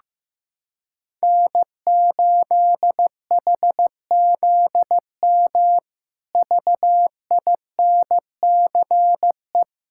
Morse code, heard as a keyed tone, about 15 wpm, 700 Hz. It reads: N8HZM VINCE
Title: K1USN SST QSO Practice - 15wpm